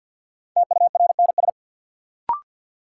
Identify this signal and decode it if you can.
Morse code, heard as a keyed tone, about 47 wpm, 700 Hz. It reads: TURNS
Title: Sets of 5 Words Encore - New Word Drill - 50wpm